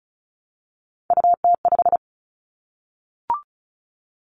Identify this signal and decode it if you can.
Morse code, heard as a keyed tone, about 36 wpm, 700 Hz. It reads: UT5
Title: Callsign Prefixes for Top 25 DXCC - 35wpm